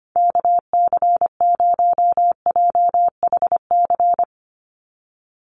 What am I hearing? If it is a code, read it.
Morse code, heard as a keyed tone, about 25 wpm, 700 Hz. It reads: KC0JHC
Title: USA Call Signs 25wpm